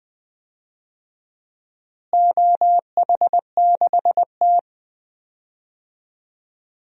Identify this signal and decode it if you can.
Morse code, heard as a keyed tone, about 20 wpm, 700 Hz. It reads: OH6T
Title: DXCC-100 Call Signs - 4 Characters - Repeated 3x - 20wpm